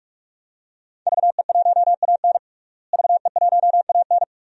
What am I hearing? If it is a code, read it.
Morse code, heard as a keyed tone, about 48 wpm, 700 Hz. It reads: VE1AN VE1AN
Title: DXCC-100 Call Signs - 5 Characters - Repeated 2x - 45wpm